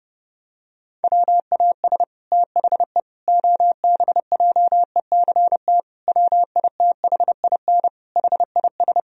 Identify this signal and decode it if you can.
Morse code, heard as a keyed tone, about 30 wpm, 700 Hz. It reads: WAS THE OBJECT WITHIN HIS
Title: Sentences from Top 1000 Words - Review Part 4 - 30wpm